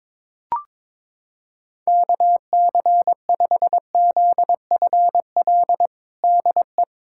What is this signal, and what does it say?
Morse code, heard as a keyed tone, about 22 wpm, 700 Hz. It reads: KC5ZFL DE
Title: USA Callsign DE Callsign - 6 Characters - 22wpm